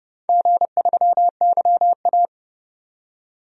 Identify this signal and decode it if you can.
Morse code, heard as a keyed tone, about 32 wpm, 700 Hz. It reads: G3YA
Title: DXCC-100 Call Signs - 4 Characters - 30wpm